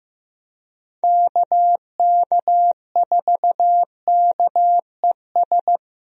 Morse code, heard as a keyed tone, about 15 wpm, 700 Hz. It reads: KK4KES
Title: USA Call Signs - 6 Characters - Repeated 2x - 15wpm